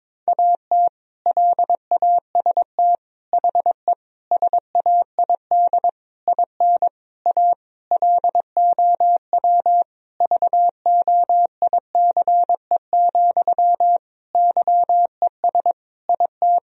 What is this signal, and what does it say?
Morse code, heard as a keyed tone, about 22 wpm, 700 Hz. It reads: AT LAST HE SAID IN A LOW VOICE, YES IT
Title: Sentences from Top 1000 Words - Review Part 1 - 22wpm